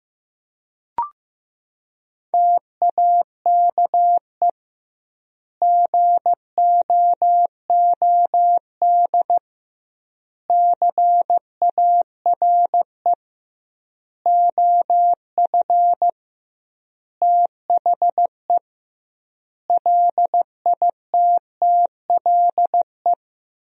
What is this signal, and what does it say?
Morse code, heard as a keyed tone, about 15 wpm, 700 Hz. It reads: TAKE GOOD CARE OF THE LITTLE
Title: 2x Word-Spacing - Sentences from Top 400 Words - 15wpm